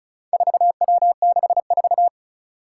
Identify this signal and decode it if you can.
Morse code, heard as a keyed tone, about 36 wpm, 700 Hz. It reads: 4W64
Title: Four Challenging Letter-Number Combinations 35wpm